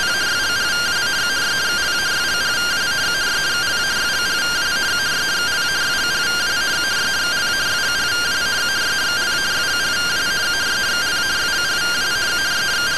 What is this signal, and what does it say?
Two-tone teleprinter signal, about 170 Hz shift.